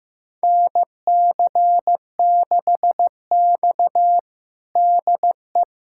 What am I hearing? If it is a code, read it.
Morse code, heard as a keyed tone, about 15 wpm, 700 Hz. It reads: NC6X DE
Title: USA Callsign DE Callsign - 4 Characters - 15wpm